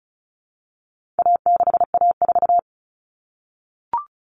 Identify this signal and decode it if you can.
Morse code, heard as a keyed tone, about 34 wpm, 700 Hz. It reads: A6A4
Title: All Cut Numbers - 4-Digit 35wpm